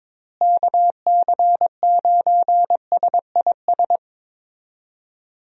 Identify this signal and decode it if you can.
Morse code, heard as a keyed tone, about 22 wpm, 700 Hz. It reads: KC9SIS
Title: USA Call Signs - 6 Characters - Repeated 2x - 22wpm